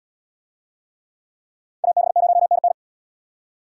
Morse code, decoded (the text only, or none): AB0NN